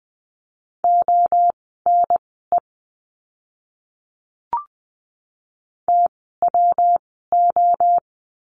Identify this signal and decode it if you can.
Morse code, heard as a keyed tone, about 20 wpm, spaced as for 14 wpm, 700 Hz. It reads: ONE TWO
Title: 146 Lesson 24f - Letter y - Words 20wpm